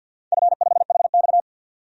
Morse code, heard as a keyed tone, about 49 wpm, 700 Hz. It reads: F5HX
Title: DXCC-100 Call Signs - 4 Characters - 50wpm